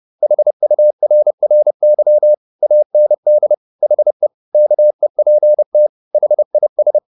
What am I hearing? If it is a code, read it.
Morse code, heard as a keyed tone, about 30 wpm, 600 Hz. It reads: HURRY AND HE KEPT HIS
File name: the old man and the sea / the old man and the sea 1 30wpm